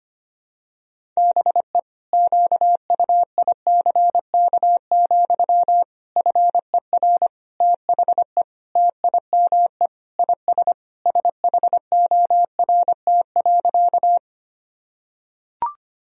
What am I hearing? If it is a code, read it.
Morse code, heard as a keyed tone, about 25 wpm, 700 Hz. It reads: BE QUICK, FER THE TIME IS SHORT.
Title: Sentences from Top 500 Words and Abbreviations 25wpm